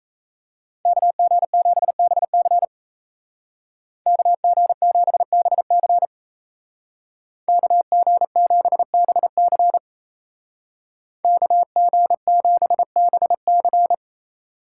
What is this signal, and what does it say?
Morse code, heard as a keyed tone, about 36 wpm, 700 Hz. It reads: KG7BC KG7BC KG7BC KG7BC